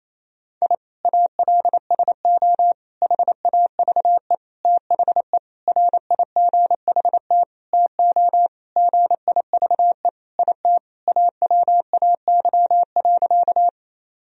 Morse code, heard as a keyed tone, about 28 wpm, 700 Hz. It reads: I ALSO HAVE THE RIGHT TO GIVE IT AWAY.
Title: Sentences from Top 1000 Words - Review Part 1 - 28wpm